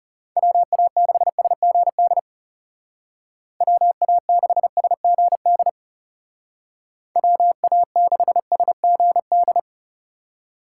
Morse code, heard as a keyed tone, about 37 wpm, 700 Hz. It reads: WA6SGD WA6SGD WA6SGD
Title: Speed Racing - USA Call Signs 30wpm